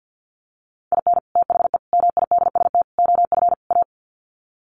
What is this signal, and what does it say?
Morse code, heard as a keyed tone, about 50 wpm, 700 Hz. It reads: IN THE MIDST OF A
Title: Sets of 5 Words 50wpm